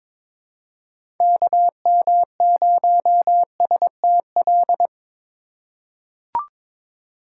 Morse code, heard as a keyed tone, about 22 wpm, 700 Hz. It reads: KM0STL
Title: USA Call Signs - 6 Characters - 4x Character Spacing - 22wpm